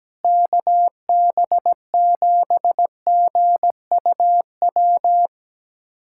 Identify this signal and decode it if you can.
Morse code, heard as a keyed tone, about 17 wpm, 700 Hz. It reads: KB7GUW
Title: USA Callsign DE Callsign - 6 Characters - 17wpm